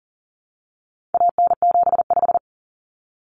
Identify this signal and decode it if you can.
Morse code, heard as a keyed tone, about 40 wpm, 700 Hz. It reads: AN75
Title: Common Cut Numbers - 4-Digit 40wpm